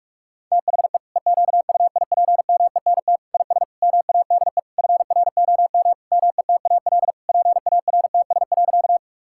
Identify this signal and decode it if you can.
Morse code, heard as a keyed tone, about 46 wpm, 700 Hz. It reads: THE EQUIPMENT IS MADE FROM METAL PARTS.
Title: Sentences from Top 1000 Words - Review Part 6 - 45wpm